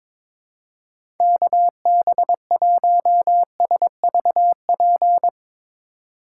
Morse code, heard as a keyed tone, about 22 wpm, 700 Hz. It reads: KB1SVP
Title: USA Call Signs - 6 Characters - 3x Character Spacing - 22wpm